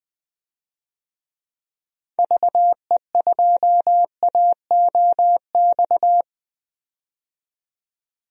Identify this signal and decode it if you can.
Morse code, heard as a keyed tone, about 20 wpm, 700 Hz. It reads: VE2AOX
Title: DXCC-100 Call Signs - 6 Characters - 20wpm